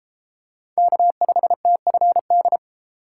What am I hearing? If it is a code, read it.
Morse code, heard as a keyed tone, about 33 wpm, 700 Hz. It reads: K5TFD